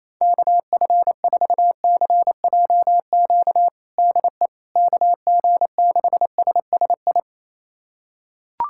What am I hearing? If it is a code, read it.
Morse code, heard as a keyed tone, about 27 wpm, 700 Hz. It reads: KF4CJQ DE KG6SSI
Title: USA Callsign DE Callsign - 6 Characters - 28wpm